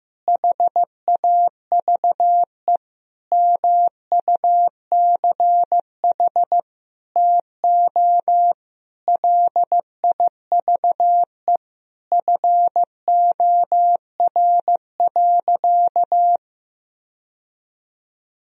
Morse code, heard as a keyed tone, about 15 wpm, 700 Hz. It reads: HAVE MUCH TO LIVE FOR.
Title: Sentences from Top 1000 Words - Review Part 3 - 15wpm